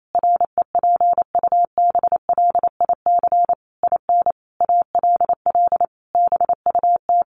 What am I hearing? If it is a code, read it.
Morse code, heard as a keyed tone, about 28 wpm, 700 Hz. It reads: REPUBLIC IN ALL BUT
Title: Sentences from Top 1500 Words 28wpm